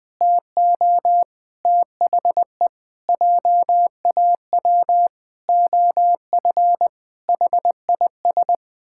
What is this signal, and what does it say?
Morse code, heard as a keyed tone, about 20 wpm, 700 Hz. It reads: TO THE JAW OF HIS
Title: Pangram Challenge v2 - 20wpm